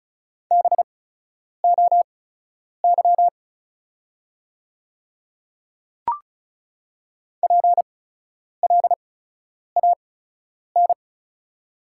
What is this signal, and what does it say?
Morse code, heard as a keyed tone, about 36 wpm, spaced as for 10 wpm, 700 Hz. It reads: BOY PLAN
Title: 8x Character Spacing - Top 500 Words - 35wpm